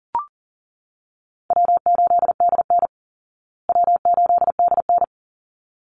Morse code, heard as a keyed tone, about 40 wpm, 700 Hz. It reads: W8DN W8DN
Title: Easy Call Signs Repeated 2x - 40wpm